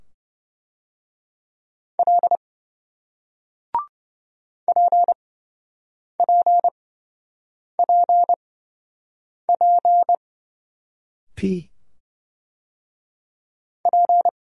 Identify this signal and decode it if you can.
Morse code, heard as a keyed tone, about 28 wpm, spaced as for 7 wpm, 700 Hz. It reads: L PPPP P